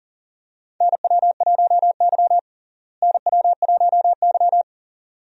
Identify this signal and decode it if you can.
Morse code, heard as a keyed tone, about 40 wpm, 700 Hz. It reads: NW1Y NW1Y
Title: USA Call Signs - 4 Characters - Repeated 2x - 40wpm